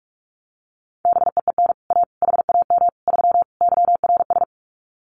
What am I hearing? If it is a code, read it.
Morse code, heard as a keyed tone, about 46 wpm, 700 Hz. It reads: BEEN A HAM 3 YRS
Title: Common QSO Phrases Encore 45wpm